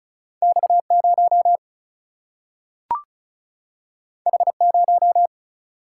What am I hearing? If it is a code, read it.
Morse code, heard as a keyed tone, about 36 wpm, 700 Hz. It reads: X 0 H 0
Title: Single-Letter Single-Number 35wpm